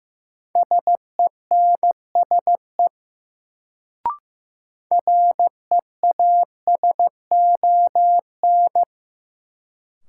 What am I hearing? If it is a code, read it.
Morse code, heard as a keyed tone, about 15 wpm, 700 Hz. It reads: SENSE REASON
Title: Sentences from Top 700 Words - New Word Drill - 15wpm